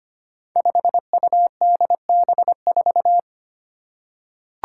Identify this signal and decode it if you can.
Morse code, heard as a keyed tone, about 25 wpm, 700 Hz. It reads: 5UDB4
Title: Five Challenging Letter-Number Combinations 25wpm